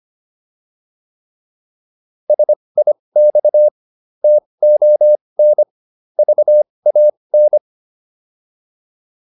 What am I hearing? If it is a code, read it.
Morse code, heard as a keyed tone, about 25 wpm, spaced as for 20 wpm, 600 Hz. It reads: SIX TON VAN